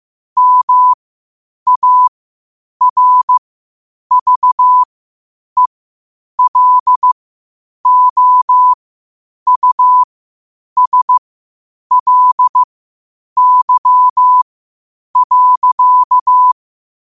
Morse code, heard as a keyed tone, about 15 wpm, spaced as for 8 wpm, 1000 Hz. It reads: MARVELOUSLY.
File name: qotdfiles / 8wpm